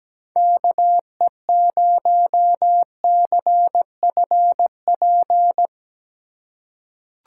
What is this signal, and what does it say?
Morse code, heard as a keyed tone, about 17 wpm, 700 Hz. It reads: KE0CFP
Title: USA Call Signs - 6 Characters - 4x Character Spacing - 17wpm